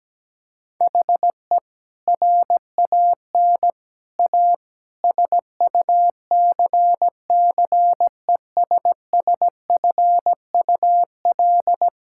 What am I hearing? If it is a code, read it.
Morse code, heard as a keyed tone, about 17 wpm, 700 Hz. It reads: HE RAN A SUCCESSFUL
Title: Sentences from Top 1000 Words - Review Part 6 - 17wpm